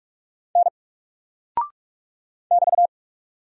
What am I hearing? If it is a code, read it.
Morse code, heard as a keyed tone, about 44 wpm, 700 Hz. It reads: N <BT>
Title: Single Character (Letter:Number:Punctuation:Prosign) Alt 45wpm